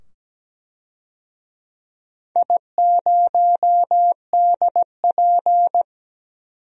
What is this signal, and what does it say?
Morse code, heard as a keyed tone, about 17 wpm, 700 Hz. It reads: I0DP
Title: DXCC-100 Call Signs - 4 Characters - 17wpm